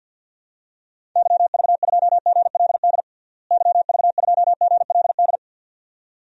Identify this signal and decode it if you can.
Morse code, heard as a keyed tone, about 52 wpm, 700 Hz. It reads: YV2GLD YV2GLD
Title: DXCC-100 Call Signs - 6 Characters - Repeated 2x - 50wpm